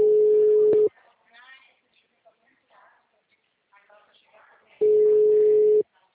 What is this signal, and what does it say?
Ringback tone (Europe, 425 Hz), with speech.